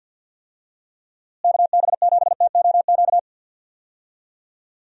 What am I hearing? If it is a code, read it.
Morse code, heard as a keyed tone, about 46 wpm, 700 Hz. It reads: KB7TOQ
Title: USA Call Signs - 6 Characters - Repeated 2x - 50wpm